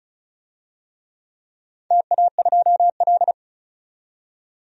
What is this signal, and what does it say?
Morse code, heard as a keyed tone, about 34 wpm, 700 Hz. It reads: TA2L
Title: DXCC-100 Call Signs - 4 Characters - 35wpm